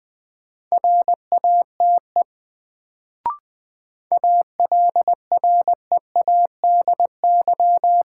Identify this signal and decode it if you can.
Morse code, heard as a keyed tone, about 20 wpm, 700 Hz. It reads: RATE ALREADY
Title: Sentences from Top 700 Words - New Word Drill - 20wpm